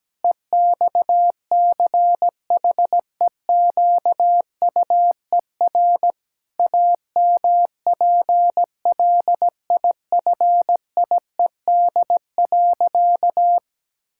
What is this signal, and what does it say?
Morse code, heard as a keyed tone, about 17 wpm, 700 Hz. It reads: EXCHEQUER AMPLIFIED.
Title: Pangram Challenge v2 - 17wpm